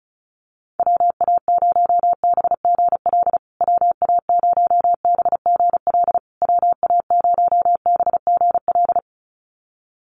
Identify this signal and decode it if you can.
Morse code, heard as a keyed tone, about 34 wpm, 700 Hz. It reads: WA0BGL WA0BGL WA0BGL
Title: USA Call Signs Repeated 3x 35wpm